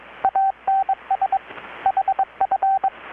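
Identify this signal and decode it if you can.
Morse code, heard as a keyed tone, about 23 wpm, 780 Hz, with radio noise. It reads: ANS HF
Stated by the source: Morse from KPH's Night of Nights XX maritime radiotelegraphy event, received in Chiapas, Mexico about 3,800 km away on a Kenwood R-600 with a grounded 10 m long-wire antenna.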